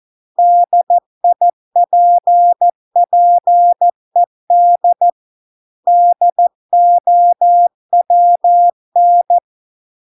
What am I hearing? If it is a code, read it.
Morse code, heard as a keyed tone, about 14 wpm, 700 Hz. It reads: DIPPED DOWN